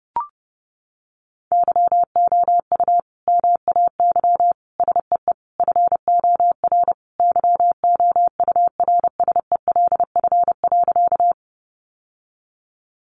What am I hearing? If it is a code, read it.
Morse code, heard as a keyed tone, about 30 wpm, 700 Hz. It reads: YOU MAY SEE FOR YOURSELF.
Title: Sentences from Top 600 Words 30wpm